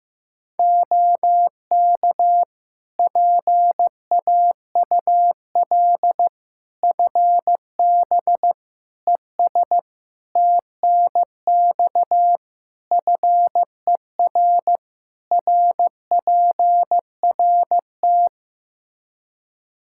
Morse code, heard as a keyed tone, about 16 wpm, 700 Hz. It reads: OK PAUL FB ES TNX FER RPRT
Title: Practice QSO - Exchange 2 - Info - 15wpm